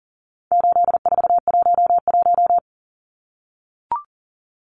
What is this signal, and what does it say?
Morse code, heard as a keyed tone, about 40 wpm, 700 Hz. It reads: 8411
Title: Numbers - 4-Digit 40wpm